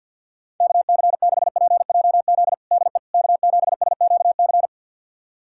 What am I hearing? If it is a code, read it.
Morse code, heard as a keyed tone, about 52 wpm, 700 Hz. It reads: KC6PJZ DE K7SQC
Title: USA Callsign DE Callsign 50wpm